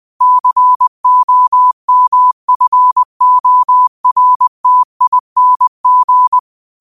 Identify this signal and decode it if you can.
Morse code, heard as a keyed tone, about 20 wpm, 1000 Hz. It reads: COMFORTING
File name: qotdfiles / 20wpm